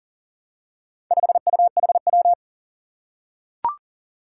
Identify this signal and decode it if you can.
Morse code, heard as a keyed tone, about 42 wpm, 700 Hz. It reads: 5UHW
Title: Four Challenging Letter-Number Combinations 40wpm